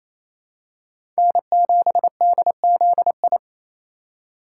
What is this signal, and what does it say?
Morse code, heard as a keyed tone, about 28 wpm, 700 Hz. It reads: N7DZI
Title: USA Call Signs - 5 Characters - 4x Character Spacing - 28wpm